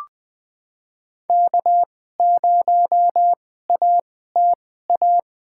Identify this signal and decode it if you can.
Morse code, heard as a keyed tone, about 20 wpm, spaced as for 14 wpm, 700 Hz. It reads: K0ATA
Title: USA Call Signs - 5 Characters - 2x Character Spacing - 20wpm